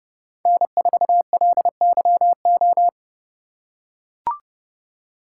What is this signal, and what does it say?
Morse code, heard as a keyed tone, about 31 wpm, 700 Hz. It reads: N4LYO